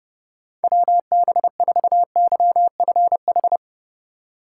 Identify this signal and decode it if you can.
Morse code, heard as a keyed tone, about 31 wpm, 700 Hz. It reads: WB4YFH
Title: USA Call Signs - 6 Characters - 4x Character Spacing - 30wpm